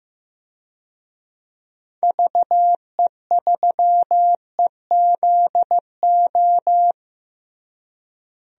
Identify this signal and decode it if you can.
Morse code, heard as a keyed tone, about 15 wpm, 700 Hz. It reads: VE3EZO
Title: DXCC-100 Call Signs - 6 Characters - Repeated 3x - 15wpm